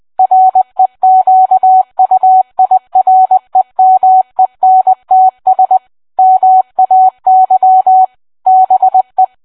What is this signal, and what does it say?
Morse code, heard as a keyed tone, about 20 wpm, 750 Hz. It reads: REQUIREMENTS MAY BE